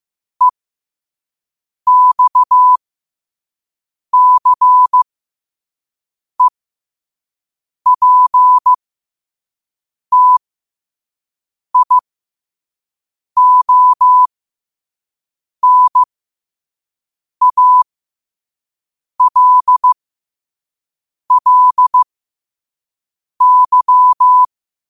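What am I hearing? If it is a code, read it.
Morse code, heard as a keyed tone, about 15 wpm, spaced as for 5 wpm, 1000 Hz. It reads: EXCEPTIONALLY
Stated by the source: Morse code quote of the day at 5 wpm.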